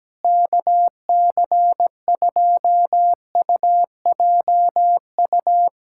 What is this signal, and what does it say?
Morse code, heard as a keyed tone, about 17 wpm, 700 Hz. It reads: KC2UJU
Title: USA Callsign DE Callsign - 6 Characters - 17wpm